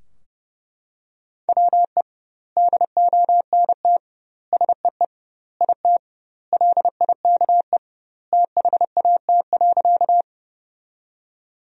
Morse code, heard as a keyed tone, about 32 wpm, 700 Hz. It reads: WE DONT SEE IT LIKE THAT.
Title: 2x Word-Spacing - Sentences from Top 300 Words - 30wpm